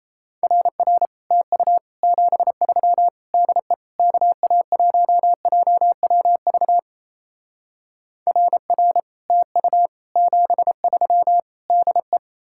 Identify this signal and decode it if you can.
Morse code, heard as a keyed tone, about 37 wpm, 700 Hz. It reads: RR TU 73 DE KA1JWV RR TU 73 DE